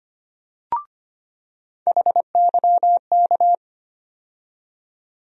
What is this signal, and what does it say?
Morse code, heard as a keyed tone, about 25 wpm, 700 Hz. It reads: HYK
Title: USA Call Sign Postfixes - 3 Letters 25wpm